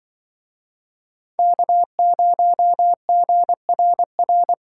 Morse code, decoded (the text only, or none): K0GRR